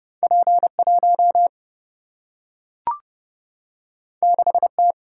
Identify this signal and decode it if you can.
Morse code, heard as a keyed tone, about 30 wpm, 700 Hz. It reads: P1 6T
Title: Single-Letter Single-Number 30wpm